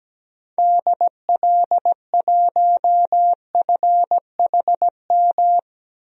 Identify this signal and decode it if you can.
Morse code, heard as a keyed tone, about 17 wpm, 700 Hz. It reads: DL1FHM
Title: DXCC-100 Call Signs - 6 Characters - 17wpm